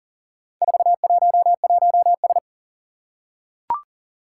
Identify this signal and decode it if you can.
Morse code, heard as a keyed tone, about 40 wpm, 700 Hz. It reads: 411S
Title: Four Challenging Letter-Number Combinations 40wpm